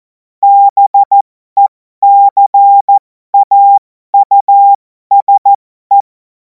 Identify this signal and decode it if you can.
Morse code, heard as a keyed tone, about 14 wpm, spaced as for 12 wpm, 800 Hz. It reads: BECAUSE
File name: Morsekurs / Witze